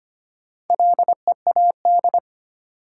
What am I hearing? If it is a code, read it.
Morse code, heard as a keyed tone, about 25 wpm, 700 Hz. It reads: LEAD
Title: Four Letter Words - 25wpm